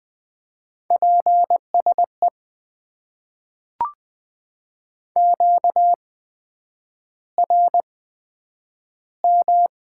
Morse code, heard as a keyed tone, about 20 wpm, spaced as for 9 wpm, 700 Hz. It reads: PSE QRM
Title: Top 90 QSO Elements - 8x Character Spacing - 20wpm